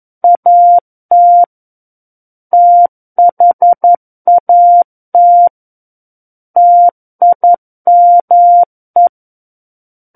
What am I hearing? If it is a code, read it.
Morse code, heard as a keyed tone, about 11 wpm, 700 Hz. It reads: AT THAT TIME
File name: Book One / 02-Chapter 2 - 11 WPM